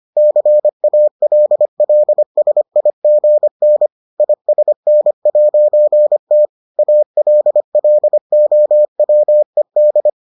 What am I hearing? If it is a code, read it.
Morse code, heard as a keyed tone, about 25 wpm, 600 Hz. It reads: CALLSIGN ISN'T ALLOWED